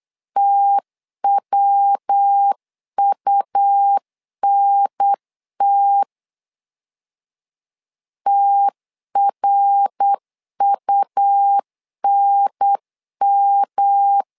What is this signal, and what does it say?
Morse code, heard as a keyed tone, about 8 wpm, 800 Hz. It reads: TWUNT TRUNM
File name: LEZIONI / 3 MWRO / 3 MWR 8wpm 2444